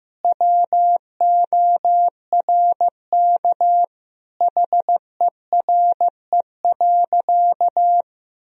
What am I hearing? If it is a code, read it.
Morse code, heard as a keyed tone, about 16 wpm, 700 Hz. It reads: WORK HERE.
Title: Sentences from Top 1000 Words - Review Part 4 - 15wpm